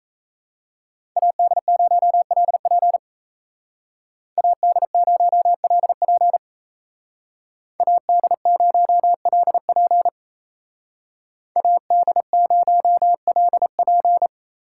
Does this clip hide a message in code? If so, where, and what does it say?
Yes: Morse code from 1.2 to 14.3 s, heard as a keyed tone, about 36 wpm, 700 Hz. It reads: AD0LP AD0LP AD0LP AD0LP